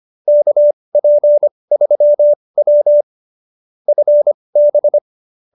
Morse code, heard as a keyed tone, about 25 wpm, spaced as for 20 wpm, 570 Hz. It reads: KP3W FB